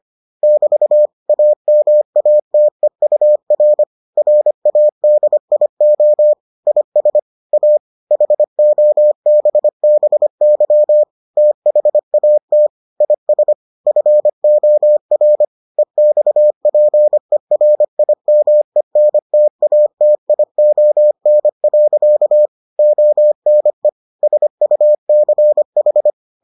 Morse code, heard as a keyed tone, about 25 wpm, 600 Hz. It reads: <BT> AMATEUR RADIO IS A HOBBY THAT IS FOR EXPERIMENTATION. ONE SUCH